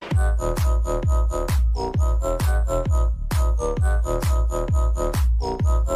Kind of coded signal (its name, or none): none